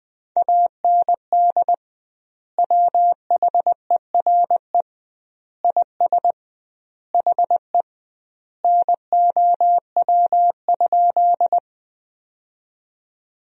Morse code, heard as a keyed tone, about 20 wpm, 700 Hz. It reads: AND WHERE IS HE NOW?
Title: 2x Word-Spacing - Sentences from Top 200 Words - 20wpm